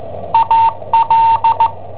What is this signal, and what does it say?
Morse code, heard as a keyed tone, about 16 wpm, 910 Hz. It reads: AL
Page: Cassetta esercizi preparata da I7OHP per IW5ELP